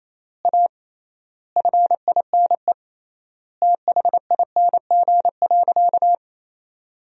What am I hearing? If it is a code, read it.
Morse code, heard as a keyed tone, about 28 wpm, 700 Hz. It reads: A FINE THING.
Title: 3x Word-Spacing - Sentences from Top 500 Words - 28wpm